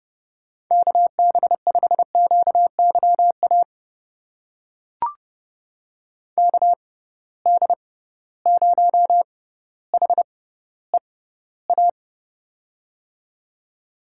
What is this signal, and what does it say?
Morse code, heard as a keyed tone, about 32 wpm, spaced as for 16 wpm, 700 Hz. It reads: KB5QYA KD0HEA
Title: USA Call Signs - 6 Characters - 6x Character Spacing - 30wpm